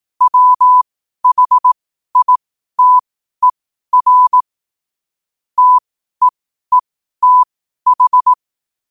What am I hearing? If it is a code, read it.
Morse code, heard as a keyed tone, about 18 wpm, spaced as for 12 wpm, 1000 Hz. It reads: WHITER TEETH
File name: qotdfiles / 12wpm